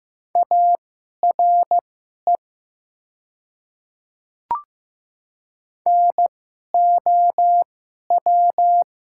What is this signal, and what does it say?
Morse code, heard as a keyed tone, about 15 wpm, spaced as for 11 wpm, 700 Hz. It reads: ARE NOW
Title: 2x Character Spacing - Top 100 Words - 15wpm